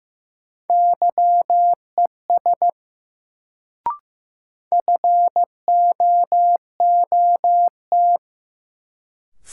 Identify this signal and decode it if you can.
Morse code, heard as a keyed tone, about 15 wpm, 700 Hz. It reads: YES FOOT
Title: Sentences from Top 500 Words - New Word Drill - 15wpm